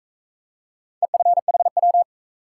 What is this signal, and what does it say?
Morse code, heard as a keyed tone, about 41 wpm, 700 Hz. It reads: EFHW